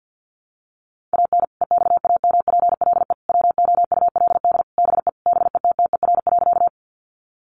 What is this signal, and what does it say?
Morse code, heard as a keyed tone, about 50 wpm, 700 Hz. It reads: AN EXAMPLE WOULD BE BETTER.
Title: Sentences from Top 400 Words 50wpm